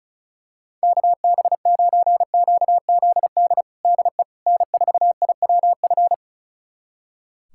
Morse code, heard as a keyed tone, about 36 wpm, 700 Hz. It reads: KB9QZD DE N4IWF
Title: USA Callsign DE Callsign 35wpm